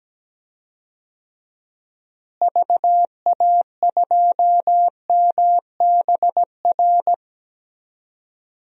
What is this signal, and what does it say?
Morse code, heard as a keyed tone, about 17 wpm, 700 Hz. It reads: VA2MBR
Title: DXCC-100 Call Signs - 6 Characters - Repeated 3x - 17wpm